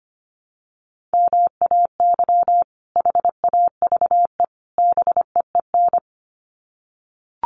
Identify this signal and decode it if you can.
Morse code, heard as a keyed tone, about 25 wpm, 700 Hz. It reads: MAY HAVE BEEN
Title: Sets of 3 Words 25wpm